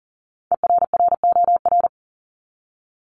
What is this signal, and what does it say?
Morse code, heard as a keyed tone, about 40 wpm, 700 Hz. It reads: ERROR
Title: Top 1751-2000 Words 40wpm